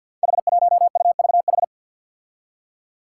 Morse code, heard as a keyed tone, about 46 wpm, 700 Hz. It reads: H1UVH
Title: Five Challenging Letter-Number Combinations 50wpm